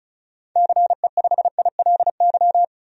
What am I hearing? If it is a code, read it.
Morse code, heard as a keyed tone, about 36 wpm, 700 Hz. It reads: CE5ILY
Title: DXCC-100 Call Signs - 6 Characters - 35wpm